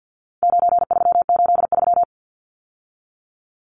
Morse code, heard as a keyed tone, about 50 wpm, 700 Hz. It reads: 9383
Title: Numbers - 4-Digit 50wpm